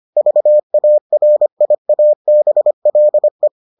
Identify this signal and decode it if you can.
Morse code, heard as a keyed tone, about 25 wpm, 600 Hz. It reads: VARIABLE